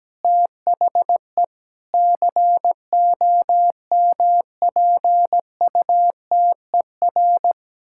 Morse code, heard as a keyed tone, about 17 wpm, 700 Hz. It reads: THE COMPUTER
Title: Sentences from Top 1000 Words - Review Part 6 - 17wpm